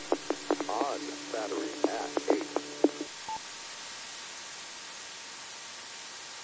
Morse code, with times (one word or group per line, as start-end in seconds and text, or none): none